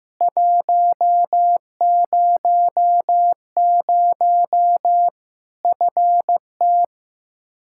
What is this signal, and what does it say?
Morse code, heard as a keyed tone, about 15 wpm, 700 Hz. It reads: 100 FT
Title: Practice QSO - Exchange 2 - Info - 15wpm